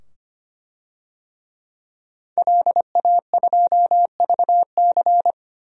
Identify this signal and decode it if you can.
Morse code, heard as a keyed tone, about 25 wpm, 700 Hz. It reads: LA2VC
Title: DXCC-100 Call Signs - 5 Characters - 25wpm